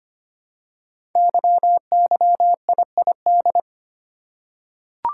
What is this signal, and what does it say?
Morse code, heard as a keyed tone, about 25 wpm, 700 Hz. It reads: YYIID
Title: Code Groups - 5 Letter:Numbers - 25wpm